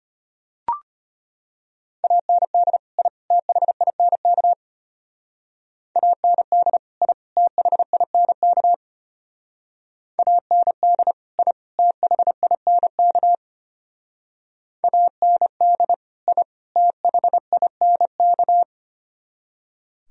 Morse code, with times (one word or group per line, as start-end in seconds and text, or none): 2.0-2.8 AND
3.0-3.1 I
3.3-4.5 THINK
6.0-6.8 AND
7.0-7.1 I
7.4-8.7 THINK
10.2-11.1 AND
11.4-11.5 I
11.8-13.3 THINK
14.8-15.9 AND
16.3-16.4 I
16.8-18.6 THINK